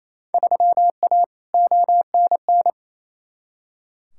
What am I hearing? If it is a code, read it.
Morse code, heard as a keyed tone, about 28 wpm, 700 Hz. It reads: 3A ONN
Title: Field Day Exchanges Encore - 28wpm